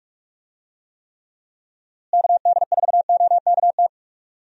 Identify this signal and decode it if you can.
Morse code, heard as a keyed tone, about 43 wpm, 700 Hz. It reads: KD4OKT
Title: USA Call Signs - 6 Characters - Repeated 3x - 45wpm